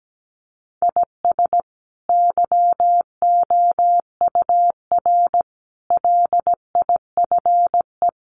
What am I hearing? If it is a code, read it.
Morse code, heard as a keyed tone, about 17 wpm, 700 Hz. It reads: IS YOUR LIFE
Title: Sentences from Top 1500 Words 17wpm